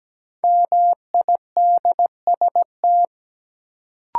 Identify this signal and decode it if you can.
Morse code, heard as a keyed tone, about 17 wpm, 700 Hz. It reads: MIDST
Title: Sets of 4 Words - New Word Drill - 17wpm